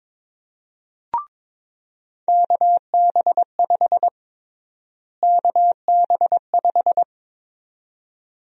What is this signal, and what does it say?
Morse code, heard as a keyed tone, about 21 wpm, 700 Hz. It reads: KB5 KB5
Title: USA Call Sign Prefixes - Repeated 2x - 22wpm